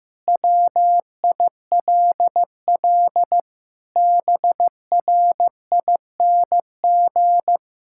Morse code, heard as a keyed tone, about 15 wpm, 700 Hz. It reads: WILL BRING
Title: Sentences from Top 1000 Words - Review Part 6 - 15wpm